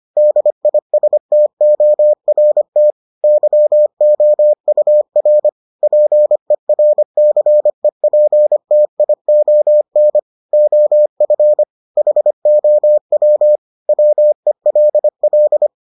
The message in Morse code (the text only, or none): DISTORT YOUR PERCEPTION OF HOW WELL